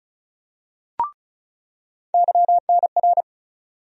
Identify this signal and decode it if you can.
Morse code, heard as a keyed tone, about 34 wpm, 700 Hz. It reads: YNR
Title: USA Call Sign Postfixes - 3 Letters 35wpm